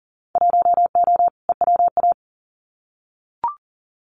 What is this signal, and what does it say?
Morse code, heard as a keyed tone, about 38 wpm, 700 Hz. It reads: 1O EWA
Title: Winter Field Day Exchanges Encore 40wpm